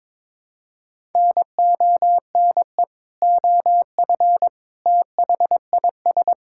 Morse code, heard as a keyed tone, about 22 wpm, 700 Hz. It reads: NONE OF THIS
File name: practice / 22wpm / Sets of 7 Words - 22wpm